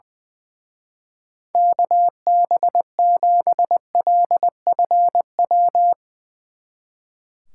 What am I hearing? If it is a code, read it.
Morse code, heard as a keyed tone, about 20 wpm, 700 Hz. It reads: KB7LFW
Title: Speed Racing - USA 2x3 Call Signs 20wpm